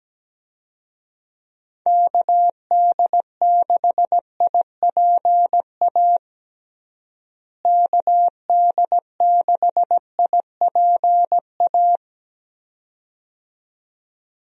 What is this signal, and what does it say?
Morse code, heard as a keyed tone, about 17 wpm, 700 Hz. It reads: KD6IPA KD6IPA
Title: USA Call Signs - 6 Characters - Repeated 2x - 17wpm